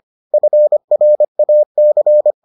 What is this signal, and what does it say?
Morse code, heard as a keyed tone, about 25 wpm, 600 Hz. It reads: FRAC